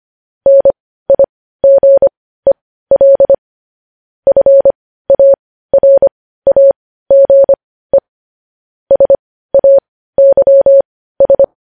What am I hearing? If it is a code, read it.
Morse code, heard as a keyed tone, about 25 wpm, spaced as for 15 wpm, 550 Hz. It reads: NIGEL FARAGE SAYS
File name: podcast / News headlines-1x-25x15